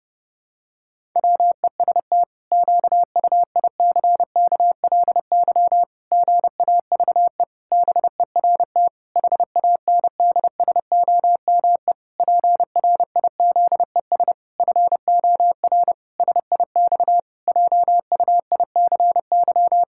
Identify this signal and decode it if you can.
Morse code, heard as a keyed tone, about 32 wpm, 700 Hz. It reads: WEST QUICKLY GAVE BERT HANDSOME PRIZES FOR SIX JUICY
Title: Pangram Challenge v2 - 30wpm